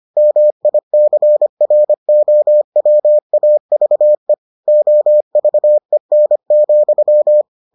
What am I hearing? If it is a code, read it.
Morse code, heard as a keyed tone, about 25 wpm, 600 Hz. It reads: MICROWAVE OVEN,